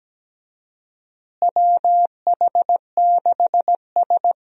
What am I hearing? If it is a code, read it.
Morse code, heard as a keyed tone, about 17 wpm, 700 Hz. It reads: WH6S
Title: USA Call Signs - 4 Characters - Repeated 2x - 17wpm